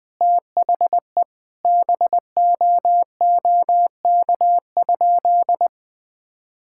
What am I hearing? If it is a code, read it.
Morse code, heard as a keyed tone, about 20 wpm, 700 Hz. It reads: THE BOOK?
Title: Sentences from Top 1000 Words - Review Part 5 - 20wpm